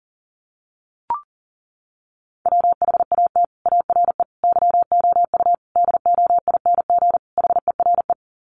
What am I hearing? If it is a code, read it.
Morse code, heard as a keyed tone, about 40 wpm, 700 Hz. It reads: WHAT ARE YOU DOING HERE
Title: Sets of 5 Words 40wpm